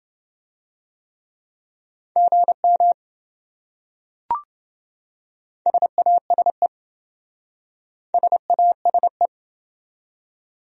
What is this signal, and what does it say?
Morse code, heard as a keyed tone, about 27 wpm, 700 Hz. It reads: GM SASE SASE